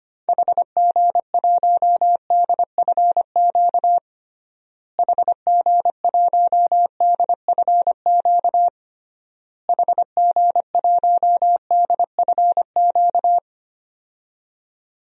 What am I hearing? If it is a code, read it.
Morse code, heard as a keyed tone, about 25 wpm, 700 Hz. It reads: HG1DFQ HG1DFQ HG1DFQ
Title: DXCC-100 Call Signs - 6 Characters - Repeated 3x - 25wpm